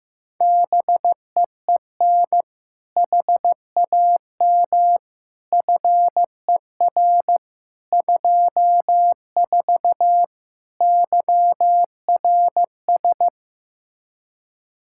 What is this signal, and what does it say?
Morse code, heard as a keyed tone, about 15 wpm, 700 Hz. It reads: BEEN HAM FER 24 YRS
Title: Practice QSO - Exchange 3 - More Info - 15wpm